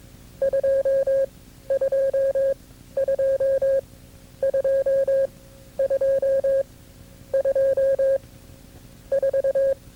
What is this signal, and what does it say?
Morse code, heard as a keyed tone, about 22 wpm, spaced as for 11 wpm, 540 Hz. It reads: 222222 4
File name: CW-practice/KB6MT-tape2-side1